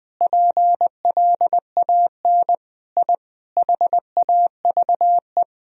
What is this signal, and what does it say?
Morse code, heard as a keyed tone, about 21 wpm, 700 Hz. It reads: PLAN I HAVE
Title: Sentences from Top 1000 Words - Review Part 2 - 20wpm